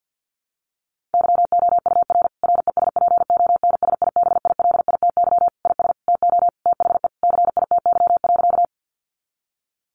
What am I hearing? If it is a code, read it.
Morse code, heard as a keyed tone, about 50 wpm, 700 Hz. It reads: YOUR RESPONSIBILITY IS TO THE CITY.
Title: Sentences from Top 1200 Words 50wpm